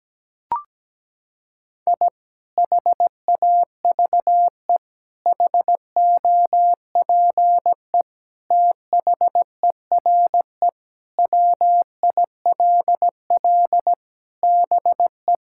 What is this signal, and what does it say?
Morse code, heard as a keyed tone, about 17 wpm, 700 Hz. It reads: I HAVE HOPE THERE WILL BE
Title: Sentences from Top 1000 Words - Review Part 5 - 17wpm